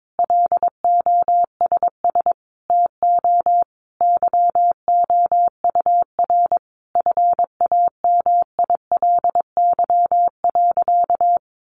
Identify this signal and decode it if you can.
Morse code, heard as a keyed tone, about 22 wpm, 700 Hz. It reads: LOSS TO YOUR FAMILY.
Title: Sentences from Top 1400 Words 22wpm